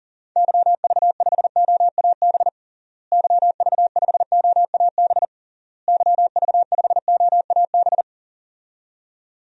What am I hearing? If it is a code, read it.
Morse code, heard as a keyed tone, about 40 wpm, 700 Hz. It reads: YV5OAB YV5OAB YV5OAB
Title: DXCC-100 Call Signs - 6 Characters - Repeated 3x - 40wpm